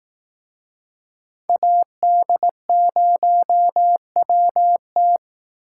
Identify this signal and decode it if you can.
Morse code, heard as a keyed tone, about 18 wpm, 700 Hz. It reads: AD0WT
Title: Speed Racing - USA 2x2 Call Signs 15wpm